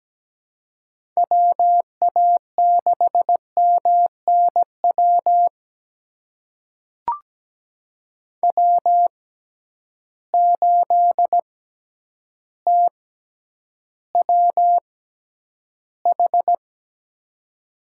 Morse code, heard as a keyed tone, about 17 wpm, spaced as for 9 wpm, 700 Hz. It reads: WA6MNW W8TWH
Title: USA Call Signs - 6x Character Spacing - 17wpm